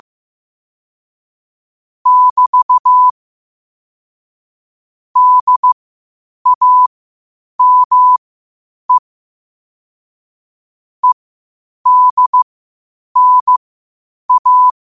Morse code, heard as a keyed tone, about 15 wpm, spaced as for 8 wpm, 1000 Hz. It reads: <BT> DAME EDNA
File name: qotdfiles / 8wpm